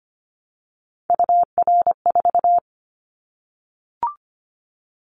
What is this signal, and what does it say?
Morse code, heard as a keyed tone, about 25 wpm, 700 Hz. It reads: UR4
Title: Callsign Prefixes for Top 25 DXCC - 25wpm